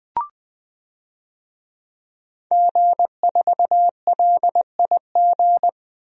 Morse code, heard as a keyed tone, about 20 wpm, 700 Hz. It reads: G4LIG
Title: DXCC-100 Call Signs - 5 Characters - 20wpm